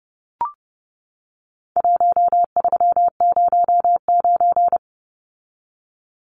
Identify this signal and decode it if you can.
Morse code, heard as a keyed tone, about 30 wpm, 700 Hz. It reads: 1309
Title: Numbers - 4-Digit 30wpm